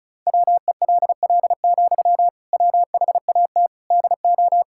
Morse code, heard as a keyed tone, about 35 wpm, 700 Hz. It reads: WELL, WHAT DO
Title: Sentences from Top 1600 Words 35wpm